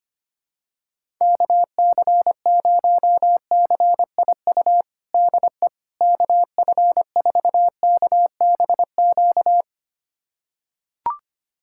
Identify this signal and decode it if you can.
Morse code, heard as a keyed tone, about 25 wpm, 700 Hz. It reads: KC0CIU DE KF4KBQ
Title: USA Callsign DE Callsign - 6 Characters - 25wpm